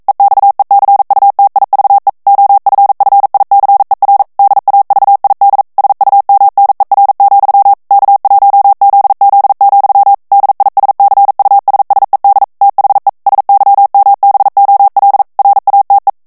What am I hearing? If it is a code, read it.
Morse code, heard as a keyed tone, about 42 wpm, 800 Hz. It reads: EXECUTIVE OFFICER DAVID SUMNER, K1ZZ, DISCUSSED THE SYMBOL RATE